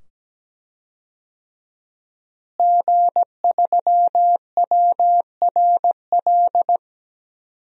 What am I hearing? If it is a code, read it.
Morse code, heard as a keyed tone, about 17 wpm, 700 Hz. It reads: G3WRL
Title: DXCC-100 Call Signs - 5 Characters - Repeated 2x - 17wpm